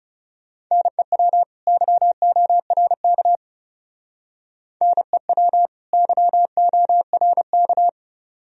Morse code, heard as a keyed tone, about 34 wpm, 700 Hz. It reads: NEW YORK NEW YORK
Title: Speed-Racing - Sets of 2 Words 30wpm